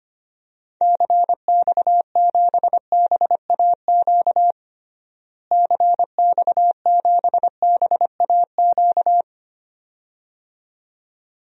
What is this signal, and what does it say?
Morse code, heard as a keyed tone, about 25 wpm, 700 Hz. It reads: CX7BAQ CX7BAQ
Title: DXCC-100 Call Signs - 6 Characters - Repeated 2x - 25wpm